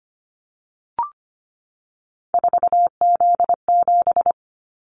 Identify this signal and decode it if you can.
Morse code, heard as a keyed tone, about 25 wpm, 700 Hz. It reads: 4Z7
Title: Callsign Prefixes for Top 50 DXCC - 25wpm